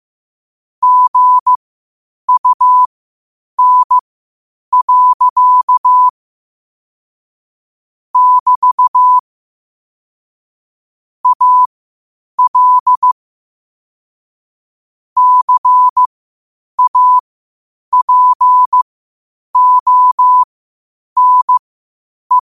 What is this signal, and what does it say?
Morse code, heard as a keyed tone, about 15 wpm, spaced as for 8 wpm, 1000 Hz. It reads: GUN. <BT> AL CAPONE
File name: qotdfiles / 8wpm